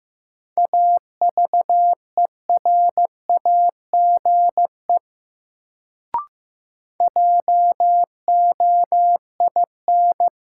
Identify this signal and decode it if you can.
Morse code, heard as a keyed tone, about 15 wpm, 700 Hz. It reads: AVERAGE JOIN
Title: Sentences from Top 1000 Words - Review Part 6 - New Word Drill - 15wpm